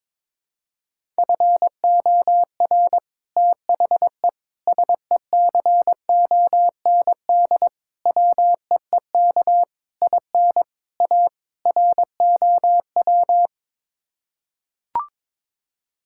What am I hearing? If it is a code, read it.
Morse code, heard as a keyed tone, about 22 wpm, 700 Hz. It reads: FOR THE SECOND WEEK IN A ROW
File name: practice / 22wpm / Sets of 7 Words - 22wpm